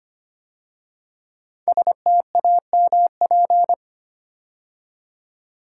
Morse code, heard as a keyed tone, about 25 wpm, 700 Hz. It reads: STAMP
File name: 5 Letter Words - First 3 Letters Given - 25wpm